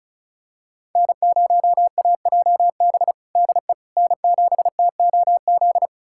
Morse code, heard as a keyed tone, about 36 wpm, 700 Hz. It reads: N0AJB DE N7TOZ
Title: USA Callsign DE Callsign - 5 Characters - 35wpm